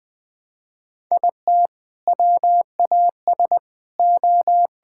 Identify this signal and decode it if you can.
Morse code, heard as a keyed tone, about 21 wpm, 700 Hz. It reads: IT WAS O
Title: Sentences with 5-Word Phrases - 20wpm